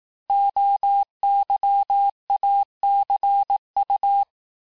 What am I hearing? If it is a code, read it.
Morse code, heard as a keyed tone, about 18 wpm, 800 Hz. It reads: OYACU
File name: cours-du-lundi files / cours-1588-texte-chiffre-vitesse-17MPM